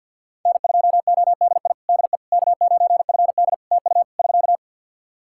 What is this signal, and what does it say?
Morse code, heard as a keyed tone, about 53 wpm, 700 Hz. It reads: N2ODI DE K9FD TU <SK>
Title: Practice QSO - Exchange 4 - Ending - 50wpm